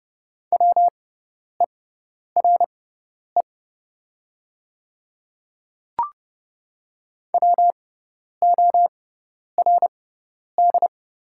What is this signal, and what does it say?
Morse code, heard as a keyed tone, about 30 wpm, spaced as for 10 wpm, 700 Hz. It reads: WERE WORD
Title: 6x Character Spacing - Top 100 Words - 30wpm